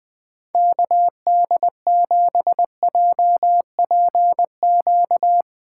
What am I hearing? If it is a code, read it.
Morse code, heard as a keyed tone, about 20 wpm, 700 Hz. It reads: KD7JPQ